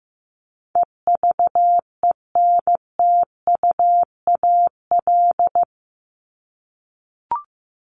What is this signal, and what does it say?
Morse code, heard as a keyed tone, about 15 wpm, 700 Hz. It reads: EVENTUAL
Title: LY Suffix - Top 100 Words with Base Words 15wpm